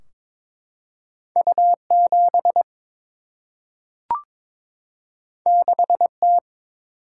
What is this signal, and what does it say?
Morse code, heard as a keyed tone, about 22 wpm, 700 Hz. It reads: U 7 6 T
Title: Single-Letter Single-Number 22wpm